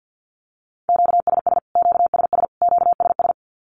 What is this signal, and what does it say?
Morse code, heard as a keyed tone, about 50 wpm, 700 Hz. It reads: QSS QSS QSS
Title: Q-Codes Repeated 3x 50wpm